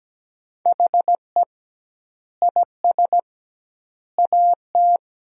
Morse code, heard as a keyed tone, about 17 wpm, 700 Hz. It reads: HE IS AT
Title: 2x Word-Spacing - Sentences from Top 300 Words - 17wpm